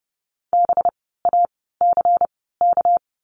Morse code, heard as a keyed tone, about 30 wpm, spaced as for 17 wpm, 700 Hz. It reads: BACK
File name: course / 30:10wpm / 230 Lesson 37f - Letter x - Words 30wpm